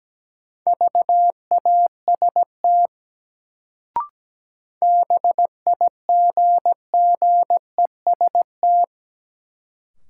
Morse code, heard as a keyed tone, about 17 wpm, 700 Hz. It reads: VAST BIGGEST
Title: Sets of 4 Words - New Word Drill - 17wpm